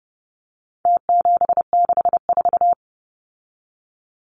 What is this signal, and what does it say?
Morse code, heard as a keyed tone, about 30 wpm, 700 Hz. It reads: T764
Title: Numbers - Contest Style - 4-Digit 30wpm